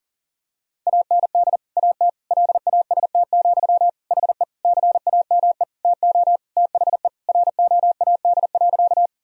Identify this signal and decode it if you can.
Morse code, heard as a keyed tone, about 40 wpm, 700 Hz. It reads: AND AT LAST, HE CAME TO THE ROAD.
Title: Sentences from Top 1000 Words - Review Part 2 - 40wpm